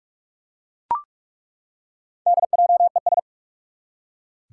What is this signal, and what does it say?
Morse code, heard as a keyed tone, about 47 wpm, 700 Hz. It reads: DJES
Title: Four Challenging Letter-Number Combinations 45wpm